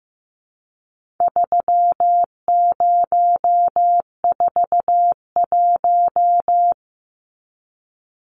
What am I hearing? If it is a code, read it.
Morse code, heard as a keyed tone, about 15 wpm, 700 Hz. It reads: 3041
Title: Numbers - 4-Digit 15wpm